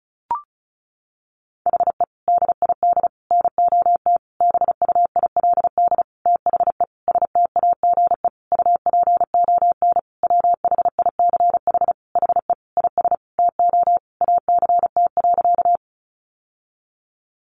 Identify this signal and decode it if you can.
Morse code, heard as a keyed tone, about 34 wpm, 700 Hz. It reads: HE DID NOT BUILD THE STAGE UPON WHICH HE IS TO ACT.
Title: Sentences from Top 1200 Words 35wpm